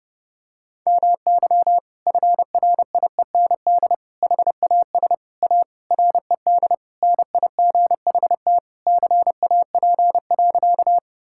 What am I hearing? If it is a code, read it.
Morse code, heard as a keyed tone, about 32 wpm, 700 Hz. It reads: MY FRIEND HAS A RED NIGHT CAP.
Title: Sentences from Top 1600 Words 30wpm